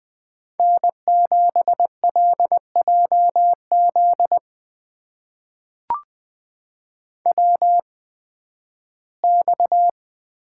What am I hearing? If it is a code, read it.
Morse code, heard as a keyed tone, about 20 wpm, spaced as for 13 wpm, 700 Hz. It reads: N7LJZ WX
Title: USA Call Signs - 5 Characters - 8x Character Spacing - 20wpm